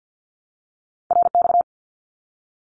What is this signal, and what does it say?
Morse code, heard as a keyed tone, about 49 wpm, 700 Hz. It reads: RX
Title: Abbreviations Repeated 3x - 50wpm